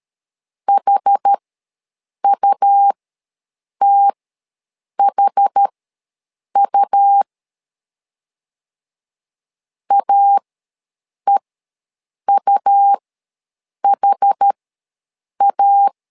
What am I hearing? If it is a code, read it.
Morse code, heard as a keyed tone, about 13 wpm, spaced as for 7 wpm, 800 Hz. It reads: HUTHU AEUHA
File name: LEZIONI / 2 UHDN / 2 UH ws 12wpm 1730